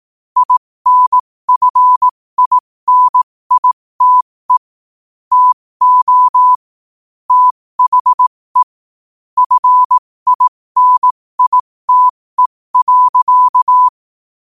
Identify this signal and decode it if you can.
Morse code, heard as a keyed tone, about 18 wpm, spaced as for 15 wpm, 1000 Hz. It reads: INFINITE TO THE FINITE.
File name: qotdfiles / 15wpm